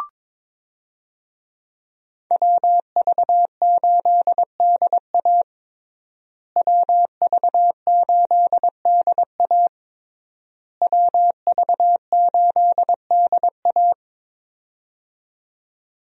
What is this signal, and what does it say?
Morse code, heard as a keyed tone, about 22 wpm, 700 Hz. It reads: WV8DA WV8DA WV8DA
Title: USA Call Signs - 5 Characters - Repeated 3x - 22wpm